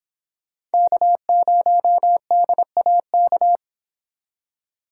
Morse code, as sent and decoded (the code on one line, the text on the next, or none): -.- ----- -.. .- -.-
K0DAK